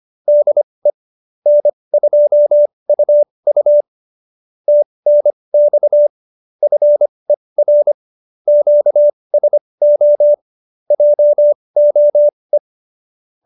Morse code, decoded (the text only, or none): DE N2UU TNX FER QSO JOE